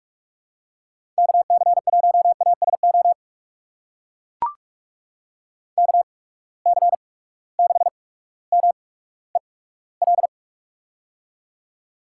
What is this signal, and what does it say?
Morse code, heard as a keyed tone, about 44 wpm, spaced as for 19 wpm, 700 Hz. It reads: KC1ASO KC6MEL
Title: USA Call Signs - 6 Characters - 8x Character Spacing - 45wpm